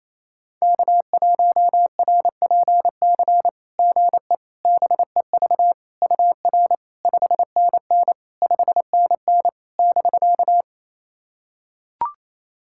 Morse code, heard as a keyed tone, about 28 wpm, 690 Hz. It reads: K1RPC GE BEV UR 5NN 5NN <BK>
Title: SOTA QSO Practice 28wpm